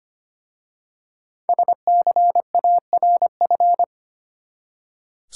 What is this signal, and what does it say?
Morse code, heard as a keyed tone, about 25 wpm, 700 Hz. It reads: SCARF
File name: practice / 25wpm / Easy Words with Hints - 25wpm